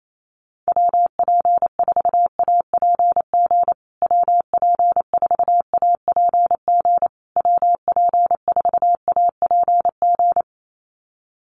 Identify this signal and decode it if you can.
Morse code, heard as a keyed tone, about 28 wpm, 700 Hz. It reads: WP4APG WP4APG WP4APG
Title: USA Call Signs Repeated 3x 28wpm